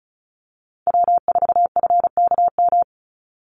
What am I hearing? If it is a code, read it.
Morse code, heard as a keyed tone, about 34 wpm, 700 Hz. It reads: W4FKM
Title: Call Signs Repeated 3x - 35wpm